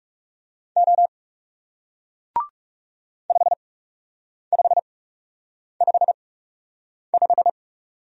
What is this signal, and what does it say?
Morse code, heard as a keyed tone, about 40 wpm, spaced as for 8 wpm, 700 Hz. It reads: O 5555